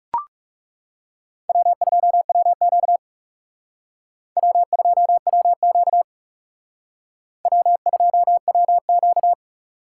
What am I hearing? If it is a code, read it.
Morse code, heard as a keyed tone, about 38 wpm, 700 Hz. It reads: W2WQ W2WQ W2WQ